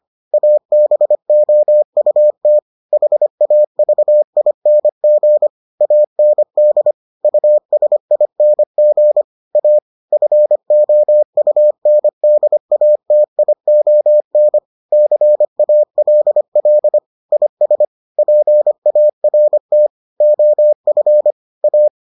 Morse code, heard as a keyed tone, about 25 wpm, 600 Hz. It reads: ABOUT HAVING AND USING A FOUNDATION CALL IS PART OF A